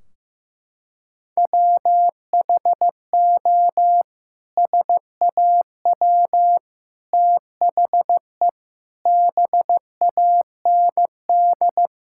Morse code, heard as a keyed tone, about 15 wpm, 700 Hz. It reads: WHO SAW THE BAND
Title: Sentences from Top 1600 Words 15wpm